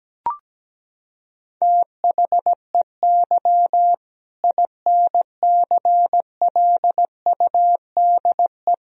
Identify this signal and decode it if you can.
Morse code, heard as a keyed tone, about 17 wpm, 700 Hz. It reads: THEY INCLUDE
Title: Sentences from Top 1000 Words - Review Part 3 - 17wpm